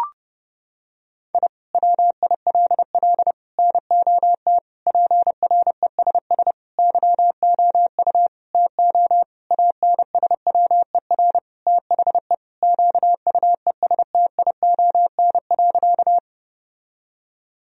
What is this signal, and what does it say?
Morse code, heard as a keyed tone, about 30 wpm, 700 Hz. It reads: I WILL NOT PRESS YOU TO ANSWER THE QUESTION.
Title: Sentences from Top 1000 Words - Review Part 1 - 30wpm